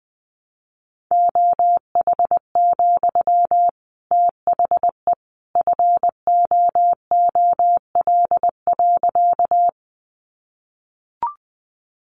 Morse code, heard as a keyed tone, about 20 wpm, 700 Hz. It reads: OH, THE FOOL.
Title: A Sound Of Thunder 20wpm